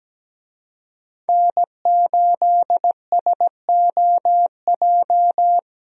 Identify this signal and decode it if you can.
Morse code, heard as a keyed tone, about 17 wpm, 700 Hz. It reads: N8SOJ
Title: USA Call Signs - 5 Characters - Repeated 2x - 17wpm